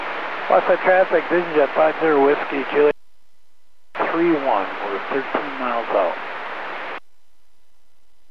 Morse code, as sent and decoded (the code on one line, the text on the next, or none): none